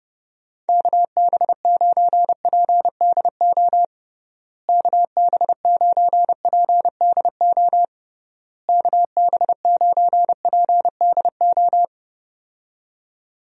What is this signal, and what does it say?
Morse code, heard as a keyed tone, about 30 wpm, 700 Hz. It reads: KB9PDO KB9PDO KB9PDO
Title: USA Call Signs - 6 Characters - Repeated 3x - 30wpm